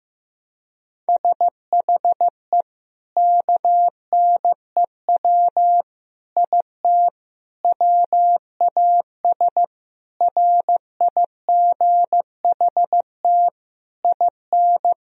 Morse code, heard as a keyed tone, about 15 wpm, 700 Hz. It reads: SHE KNEW IT WAS RIGHT IN
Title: Sentences from Top 1000 Words - Review Part 2 - 15wpm